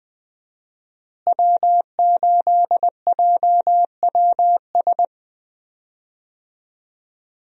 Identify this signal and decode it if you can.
Morse code, heard as a keyed tone, about 20 wpm, 700 Hz. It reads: W8JWS
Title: USA Call Signs - 5 Characters - Repeated 2x - 20wpm